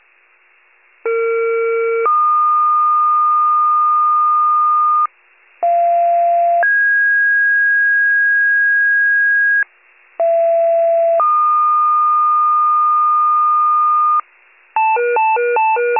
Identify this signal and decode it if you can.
Two-tone page, heard as a one-second tone, then a three-second tone, about 460 then 1220 Hz, about 690 then 1650 Hz, about 670 then 1220 Hz. It is heard from 1.0 to 14.2 s.
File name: wcalert